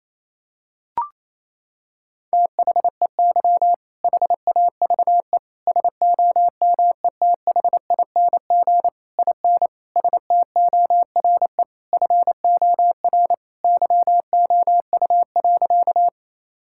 Morse code, heard as a keyed tone, about 27 wpm, 700 Hz. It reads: THEY HAVE SOMETHING IN STORE FOR YOU.
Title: Sentences from Top 1000 Words - Review Part 5 - 28wpm